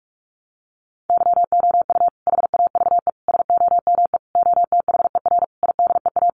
Morse code, heard as a keyed tone, about 44 wpm, 700 Hz. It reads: YOU HAVE SOME OTHER IDEA
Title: Sentences from Top 1300 Words 45wpm